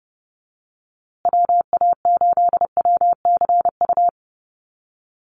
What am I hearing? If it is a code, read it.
Morse code, heard as a keyed tone, about 30 wpm, 700 Hz. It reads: WA8WCU
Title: USA Call Signs Repeated 3x 30wpm